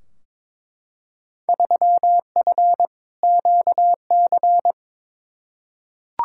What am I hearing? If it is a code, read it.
Morse code, heard as a keyed tone, about 22 wpm, 700 Hz. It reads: 3F QC
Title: Field Day Exchanges - 22wpm